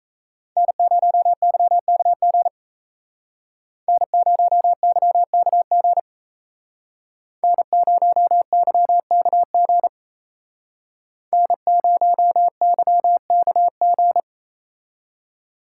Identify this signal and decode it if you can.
Morse code, heard as a keyed tone, about 34 wpm, 700 Hz. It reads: N0YKG N0YKG N0YKG N0YKG
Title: Speed Racing - USA 1x3 Call Signs 28wpm